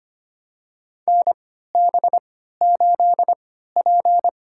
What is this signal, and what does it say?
Morse code, heard as a keyed tone, about 25 wpm, spaced as for 14 wpm, 700 Hz. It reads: NB8P
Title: USA Call Signs - 4 Characters - 3x Character Spacing - 25wpm